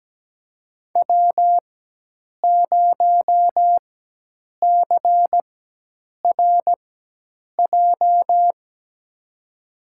Morse code, heard as a keyed tone, about 17 wpm, spaced as for 8 wpm, 700 Hz. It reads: W0CRJ
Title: USA Call Signs - 4x Character Spacing - 17wpm